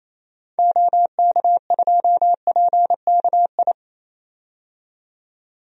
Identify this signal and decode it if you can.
Morse code, heard as a keyed tone, about 28 wpm, 700 Hz. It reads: OK2PKI
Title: DXCC-100 Call Signs - 6 Characters - 28wpm